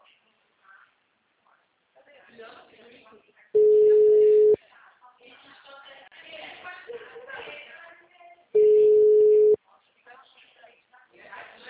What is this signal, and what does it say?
Ringback tone (Europe, 425 Hz), with speech.
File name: out- -2010- - - .28086